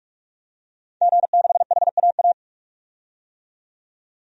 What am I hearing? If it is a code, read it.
Morse code, heard as a keyed tone, about 46 wpm, 700 Hz. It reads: G6HAA
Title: DXCC-100 Call Signs - 5 Characters - 45wpm